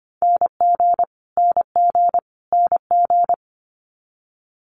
Morse code, heard as a keyed tone, about 25 wpm, 700 Hz. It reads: NG NG NG
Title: Multi-letter Phonemes Repeated 3x 25wpm